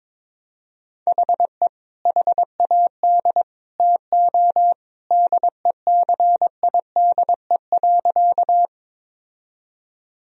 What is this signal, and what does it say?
Morse code, heard as a keyed tone, about 21 wpm, 700 Hz. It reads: HE HAD TO DECIDE.
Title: Sentences from Top 1000 Words - Review Part 3 - 22wpm